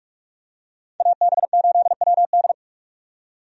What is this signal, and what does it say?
Morse code, heard as a keyed tone, about 46 wpm, 700 Hz. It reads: AB8WD
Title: USA Call Signs - 5 Characters - 45wpm